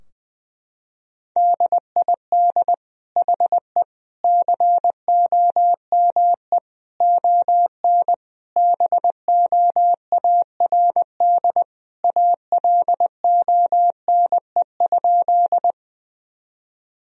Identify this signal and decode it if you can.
Morse code, heard as a keyed tone, about 21 wpm, 700 Hz. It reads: DID HE COME ON BOARD ALONE?
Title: Sentences from Top 1000 Words - Review Part 4 - 20wpm